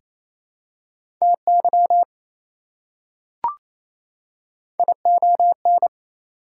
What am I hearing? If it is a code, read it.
Morse code, heard as a keyed tone, about 28 wpm, 700 Hz. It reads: TY ION
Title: Prefixes and Suffixes 28wpm